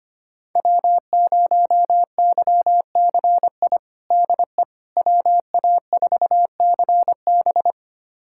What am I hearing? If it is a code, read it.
Morse code, heard as a keyed tone, about 25 wpm, 700 Hz. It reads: W0YCI DE WA4CB
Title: USA Callsign DE Callsign - 5 Characters - 25wpm